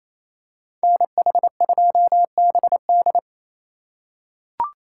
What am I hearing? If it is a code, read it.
Morse code, heard as a keyed tone, about 28 wpm, 700 Hz. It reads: NH2BD
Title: USA Call Signs - 5 Characters - 4x Character Spacing - 28wpm